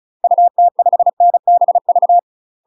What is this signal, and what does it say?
Morse code, heard as a keyed tone, about 34 wpm, 700 Hz. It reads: UT5NBV
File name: UK MASTER QSOs TRAINING / 35WPM / UK MASTER QSOs 0008